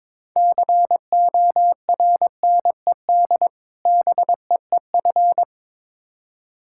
Morse code, heard as a keyed tone, about 22 wpm, 700 Hz. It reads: CORNED BEEF
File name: Difficult Words with Hints - 22wpm